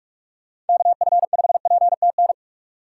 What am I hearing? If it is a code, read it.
Morse code, heard as a keyed tone, about 42 wpm, 690 Hz. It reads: KF5PTN
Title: USA Call Signs - 6 Characters - 3x Character Spacing - 45wpm